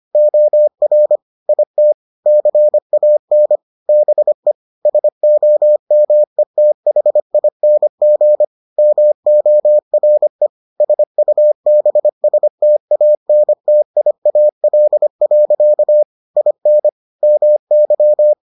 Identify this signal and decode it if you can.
Morse code, heard as a keyed tone, about 25 wpm, 600 Hz. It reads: OR IT CAN BE SOMETHING MORE SUBSTANTIAL. IN MY